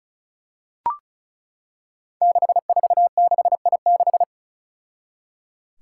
Morse code, heard as a keyed tone, about 36 wpm, 700 Hz. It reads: 646I6
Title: Five Challenging Letter-Number Combinations 35wpm